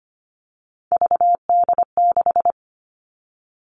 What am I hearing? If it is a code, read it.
Morse code, heard as a keyed tone, about 25 wpm, 700 Hz. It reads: VD6
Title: Three Challenging Letter-Number Combinations 25wpm